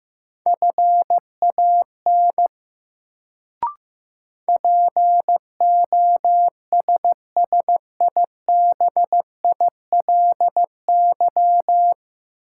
Morse code, heard as a keyed tone, about 15 wpm, 700 Hz. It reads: FAN POSSIBILY
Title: Sentences from Top 1300 Words - New Word Drill Alt - 15wpm